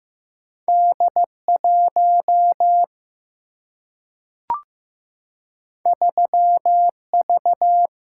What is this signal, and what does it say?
Morse code, heard as a keyed tone, about 15 wpm, 700 Hz. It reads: D 1 3 V
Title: Single-Letter Single-Number 15wpm